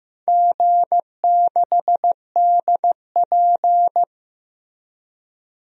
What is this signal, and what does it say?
Morse code, heard as a keyed tone, about 15 wpm, 700 Hz. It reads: G6DP
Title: DXCC-100 Call Signs - 4 Characters - Repeated 2x - 15wpm